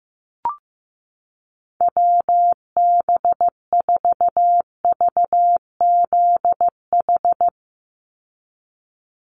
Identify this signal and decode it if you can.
Morse code, heard as a keyed tone, about 15 wpm, 700 Hz. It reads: WB4VZH
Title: USA Call Signs 15wpm